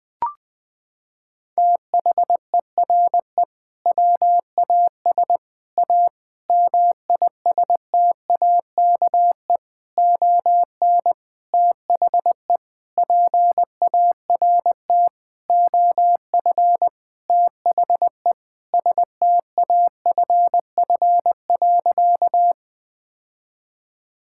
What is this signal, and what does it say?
Morse code, heard as a keyed tone, about 19 wpm, 700 Hz. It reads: THERE WAS A MISTAKE ON THE PART OF THE STAFF.
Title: Sentences with 5-Word Phrases - 20wpm